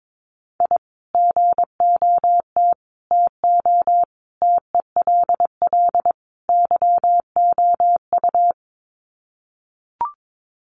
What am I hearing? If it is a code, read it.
Morse code, heard as a keyed tone, about 22 wpm, 700 Hz. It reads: I GOT TO TELL YOU
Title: Sets of 5 Words Encore2 22wpm